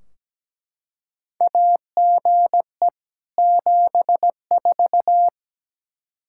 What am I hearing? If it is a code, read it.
Morse code, heard as a keyed tone, about 17 wpm, 700 Hz. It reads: AGE 74
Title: Practice QSO - Exchange 3 - More Info - 17wpm